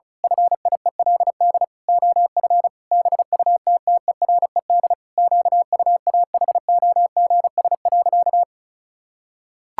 Morse code, heard as a keyed tone, about 34 wpm, 700 Hz. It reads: FIELD OF BUTTERED QUAHOGS.
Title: Pangram Challenge v2 - 35wpm